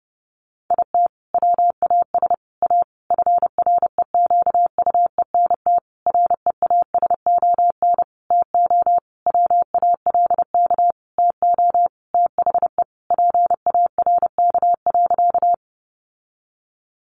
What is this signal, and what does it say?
Morse code, heard as a keyed tone, about 30 wpm, 700 Hz. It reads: IT WAS A FREQUENT REASON TO WALK TO THE PARK.
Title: Sentences from Top 1400 Words 30wpm